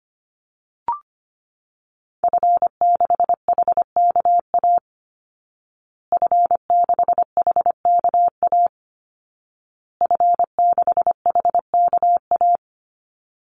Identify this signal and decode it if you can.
Morse code, heard as a keyed tone, about 25 wpm, 700 Hz. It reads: F6HKA F6HKA F6HKA
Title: Call Signs Repeated 2x - 25wpm